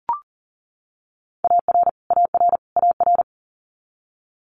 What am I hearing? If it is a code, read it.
Morse code, heard as a keyed tone, about 40 wpm, 700 Hz. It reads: AR AR AR
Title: Multi-letter Phonemes Repeated 3x 40wpm